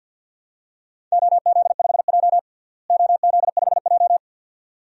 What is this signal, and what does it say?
Morse code, heard as a keyed tone, about 53 wpm, 700 Hz. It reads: OZ5J OZ5J
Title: DXCC-100 Call Signs - 4 Characters - Repeated 2x - 50wpm